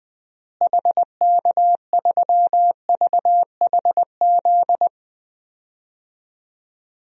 Morse code, heard as a keyed tone, about 21 wpm, 700 Hz. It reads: HK3VHZ
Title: DXCC-100 Call Signs - 6 Characters - Repeated 2x - 20wpm